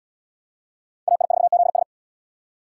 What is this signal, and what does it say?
Morse code, heard as a keyed tone, about 77 wpm, 700 Hz. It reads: AI4ZS